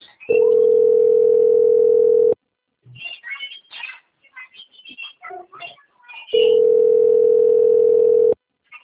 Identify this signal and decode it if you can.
Ringback tone (North America), 440 + 480 Hz, with speech.
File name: out- - - - - .9930